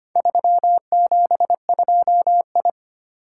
Morse code, heard as a keyed tone, about 25 wpm, 700 Hz. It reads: 372I
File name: practice / 25wpm / Four Letter-Number Combinations - 25wpm